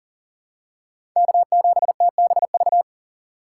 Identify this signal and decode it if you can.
Morse code, heard as a keyed tone, about 40 wpm, 700 Hz. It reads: K7TBV
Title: USA Call Signs - 5 Characters - 6x Character Spacing - 40wpm